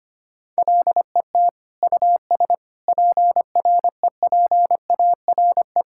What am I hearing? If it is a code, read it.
Morse code, heard as a keyed tone, about 25 wpm, 700 Hz. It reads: LET US PREPARE
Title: Sentences from Top 1000 Words - Review Part 6 - 25wpm